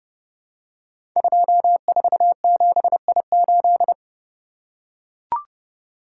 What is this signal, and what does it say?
Morse code, heard as a keyed tone, about 30 wpm, 700 Hz. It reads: 247I8
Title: Code Groups - 5 Letter:Numbers - 30wpm